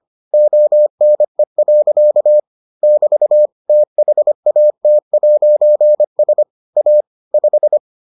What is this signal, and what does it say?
Morse code, heard as a keyed tone, about 25 wpm, 600 Hz. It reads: ONE. <BT> THAT'S A 5